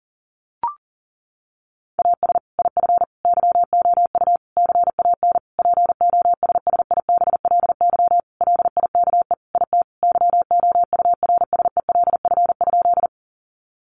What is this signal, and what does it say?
Morse code, heard as a keyed tone, about 40 wpm, 700 Hz. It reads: AS IF YOU CAN POSSIBLY LIKE IT YOURSELF?
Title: Sentences from Top 1300 Words 40wpm